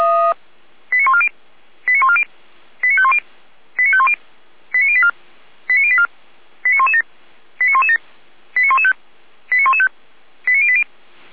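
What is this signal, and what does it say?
Five-tone selective call (ZVEI), heard as quick runs of five tones, tones of about 70 ms. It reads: 89140 89140 89510 89510 89084 89084 89117 89117 89115 89115 89080
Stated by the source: Five-tone ZVEI selective call received by SDR.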